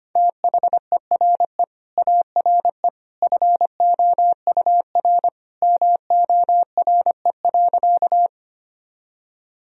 Morse code, heard as a keyed tone, about 25 wpm, 700 Hz. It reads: THERE ARE FOUR MORE.
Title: Sentences from Top 1000 Words - Review Part 1 - 25wpm